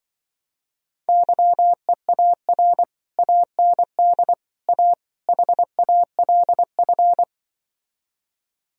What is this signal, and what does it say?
Morse code, heard as a keyed tone, about 25 wpm, 700 Hz. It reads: YEAR AND A HALF